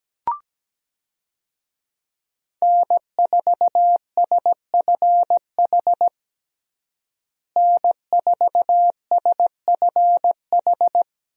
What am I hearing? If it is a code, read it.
Morse code, heard as a keyed tone, about 17 wpm, 700 Hz. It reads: N4SFH N4SFH
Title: USA Call Signs - 5 Characters - Repeated 2x - 17wpm